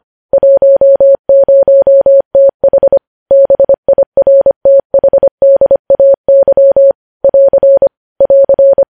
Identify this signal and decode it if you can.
Morse code, heard as a keyed tone, about 25 wpm, 550 Hz. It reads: 10TH BIRTHDAY <AR> <AR>
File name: podcast / News headlines-1x-25x25